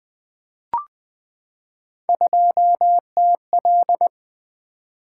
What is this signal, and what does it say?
Morse code, heard as a keyed tone, about 20 wpm, 700 Hz. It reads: 2TL
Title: Three Letter-Number Combinations - 20wpm